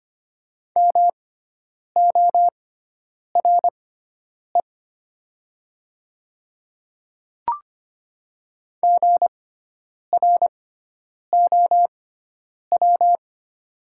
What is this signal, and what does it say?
Morse code, heard as a keyed tone, about 25 wpm, spaced as for 9 wpm, 700 Hz. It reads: MORE GROW
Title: 6x Character Spacing - Top 300 Words - 25wpm